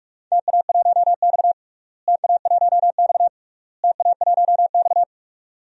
Morse code, heard as a keyed tone, about 48 wpm, 690 Hz. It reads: TA1X TA1X TA1X
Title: DXCC-100 Call Signs - 4 Characters - Repeated 3x - 45wpm